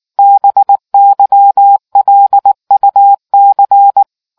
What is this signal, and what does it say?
Morse code, heard as a keyed tone, about 19 wpm, 800 Hz. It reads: BYLUC